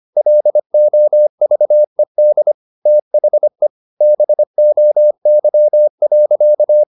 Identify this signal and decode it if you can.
Morse code, heard as a keyed tone, about 25 wpm, 600 Hz. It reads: LOVED THE BOY.
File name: the old man and the sea part 1 25wpm